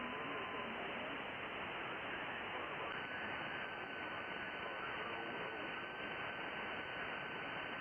FT8.
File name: contests / sept-vhf-k5tr- -2000